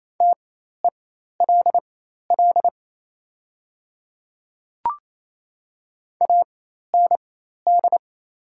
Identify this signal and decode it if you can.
Morse code, heard as a keyed tone, about 28 wpm, spaced as for 13 wpm, 690 Hz. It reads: TELL AND
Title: 4x Character Spacing - Top 200 Words - 28wpm